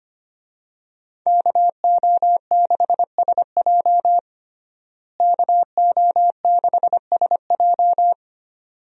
Morse code, heard as a keyed tone, about 25 wpm, 700 Hz. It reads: KO6SJ KO6SJ
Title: USA Call Signs - 5 Characters - Repeated 2x - 25wpm